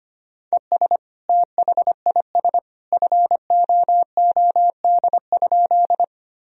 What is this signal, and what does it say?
Morse code, heard as a keyed tone, about 25 wpm, 700 Hz. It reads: ES THIS FOOD?
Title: Sentences from Top 300 Words and Abbreviations 25wpm